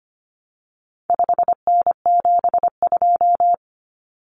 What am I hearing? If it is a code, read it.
Morse code, heard as a keyed tone, about 25 wpm, 700 Hz. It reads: 5N72
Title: Common Cut Numbers - 4-Digit 25wpm